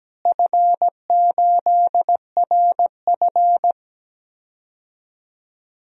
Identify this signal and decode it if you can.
Morse code, heard as a keyed tone, about 17 wpm, 700 Hz. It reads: F8RF
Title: DXCC-100 Call Signs - 4 Characters - Repeated 3x - 17wpm